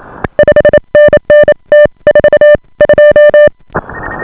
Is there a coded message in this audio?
Yes: Morse code from 0.4 to 3.5 s, heard as a keyed tone, about 28 wpm, spaced as for 24 wpm, 590 Hz. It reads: 5NN T42